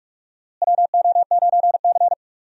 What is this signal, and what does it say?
Morse code, heard as a keyed tone, about 44 wpm, 700 Hz. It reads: WO9C
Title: USA Call Signs - 4 Characters - Repeated 3x - 45wpm